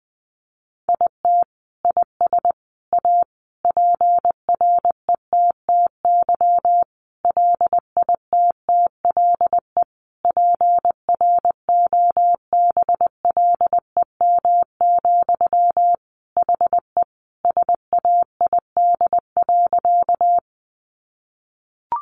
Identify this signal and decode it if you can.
Morse code, heard as a keyed tone, about 20 wpm, 700 Hz. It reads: IT IS A PRETTY LITTLE PROBLEM, HE SAID.
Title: Sentences from Top 1000 Words 20wpm